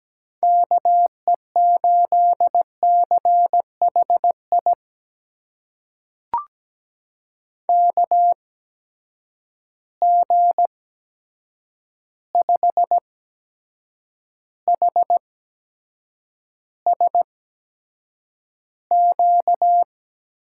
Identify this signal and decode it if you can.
Morse code, heard as a keyed tone, about 17 wpm, spaced as for 7 wpm, 700 Hz. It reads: KE8CHI KG5HSQ
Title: USA Call Signs - 6 Characters - 8x Character Spacing - 17wpm